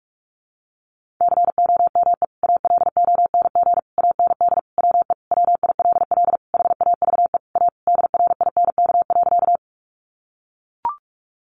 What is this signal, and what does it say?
Morse code, heard as a keyed tone, about 44 wpm, 700 Hz. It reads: COME ALONG AND WE WILL HAVE A DRINK.
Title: Sentences from Top 1500 Words 45wpm